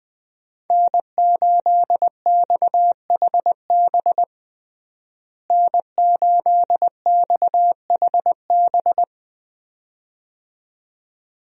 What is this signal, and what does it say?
Morse code, heard as a keyed tone, about 21 wpm, 700 Hz. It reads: N8XHB N8XHB
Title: USA Call Signs - 5 Characters - Repeated 2x - 20wpm